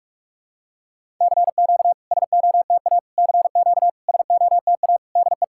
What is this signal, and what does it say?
Morse code, heard as a keyed tone, about 48 wpm, 700 Hz. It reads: CQ SOTA CQ SOTA DE
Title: SOTA QSO Practice 45wpm